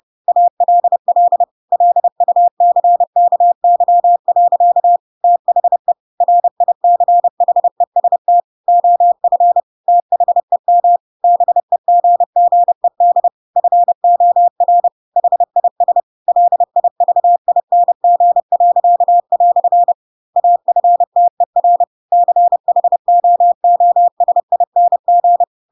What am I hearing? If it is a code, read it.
Morse code, heard as a keyed tone, about 30 wpm, 700 Hz. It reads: ALL LUCKY. THE RICHEST OF THEM BEGGED FOR HIS LIVING." AFTER CHOOSING
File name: Pinocchio / Pinocchio ch3 30 30 ik1hge